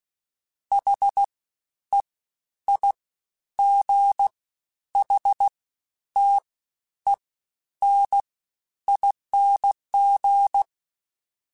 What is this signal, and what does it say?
Morse code, heard as a keyed tone, about 16 wpm, 800 Hz. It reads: H E I G H T E N ING
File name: Special INGs-mp3s / Special INGs-mp3s / Fast ing @15